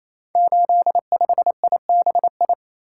Morse code, heard as a keyed tone, about 27 wpm, 700 Hz. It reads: 85IBI
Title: Code Groups - 5 Letter:Numbers - 28wpm